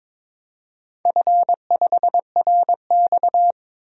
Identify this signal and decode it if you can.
Morse code, heard as a keyed tone, about 22 wpm, 700 Hz. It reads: F5RX
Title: DXCC-100 Call Signs - 4 Characters - 22wpm